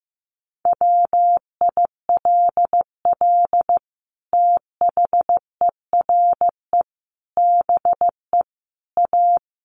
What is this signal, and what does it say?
Morse code, heard as a keyed tone, about 14 wpm, 700 Hz. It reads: WILL THERE BE A
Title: Sentences from Top 1300 Words 15wpm